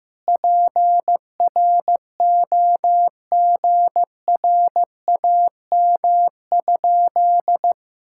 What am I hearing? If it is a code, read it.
Morse code, heard as a keyed tone, about 16 wpm, 700 Hz. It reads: PROGRAM?
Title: Sentences from Top 1000 Words - Review Part 5 - 15wpm